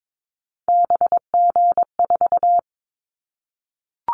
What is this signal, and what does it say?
Morse code, heard as a keyed tone, about 22 wpm, 700 Hz. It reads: BG4
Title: Callsign Prefixes for Top 100 DXCC - 22wpm